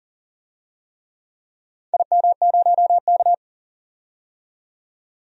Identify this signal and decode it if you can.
Morse code, heard as a keyed tone, about 40 wpm, 700 Hz. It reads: IM0K
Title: DXCC-100 Call Signs - 4 Characters - 40wpm